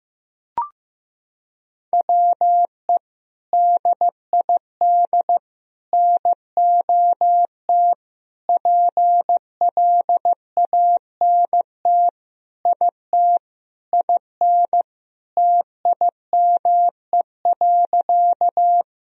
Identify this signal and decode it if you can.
Morse code, heard as a keyed tone, about 15 wpm, 700 Hz. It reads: WE DID NOT PLANT IT IN TIME.
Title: Sentences from Top 1000 Words - Review Part 1 - 15wpm